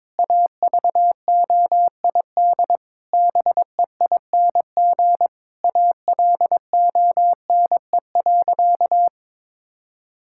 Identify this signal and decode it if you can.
Morse code, heard as a keyed tone, about 22 wpm, 700 Hz. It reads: AVOID BEING ALONE.
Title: Sentences from Top 1000 Words - Review Part 6 - 22wpm